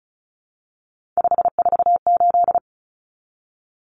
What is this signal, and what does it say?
Morse code, heard as a keyed tone, about 34 wpm, 700 Hz. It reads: 548
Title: Numbers - 3-Digit 35wpm